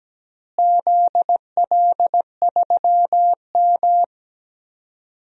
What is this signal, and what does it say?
Morse code, heard as a keyed tone, about 17 wpm, 700 Hz. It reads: ZL3M
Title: DXCC-100 Call Signs - 4 Characters - Repeated 2x - 17wpm